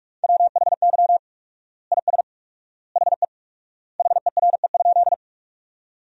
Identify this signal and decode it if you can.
Morse code, heard as a keyed tone, about 46 wpm, 700 Hz. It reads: WHY IS HE HERE?
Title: 4x Word-Spacing - Sentences from Top 200 Words - 45wpm